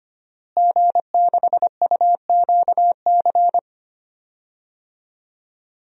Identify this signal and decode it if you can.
Morse code, heard as a keyed tone, about 25 wpm, 700 Hz. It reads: G6UQC
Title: DXCC-100 Call Signs - 5 Characters - Repeated 2x - 25wpm